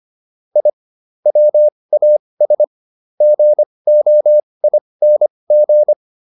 Morse code, heard as a keyed tone, about 25 wpm, spaced as for 20 wpm, 600 Hz. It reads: I WAS GOING
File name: WD304 20